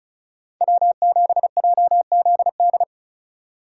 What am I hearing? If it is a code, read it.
Morse code, heard as a keyed tone, about 36 wpm, 700 Hz. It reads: W7JZD
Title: USA Call Signs - 5 Characters - 3x Character Spacing - 35wpm